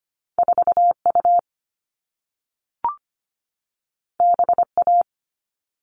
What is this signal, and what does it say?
Morse code, heard as a keyed tone, about 25 wpm, 700 Hz. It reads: 4U BA
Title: All Cut Numbers - 2-Digit 25wpm